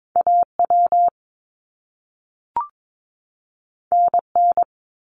Morse code, heard as a keyed tone, about 22 wpm, 700 Hz. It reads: AW NN
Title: Top Two Letter Combinations in Words 22wpm